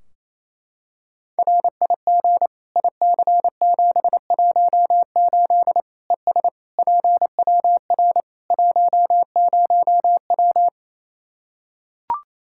Morse code, heard as a keyed tone, about 28 wpm, 700 Hz. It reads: RIG IC718 ES PWR 10W
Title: Practice QSO - Exchange 2 - Info - 28wpm